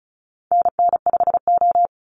Morse code, heard as a keyed tone, about 34 wpm, 700 Hz. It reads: NN5O
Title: Call Signs Repeated 2x - 35wpm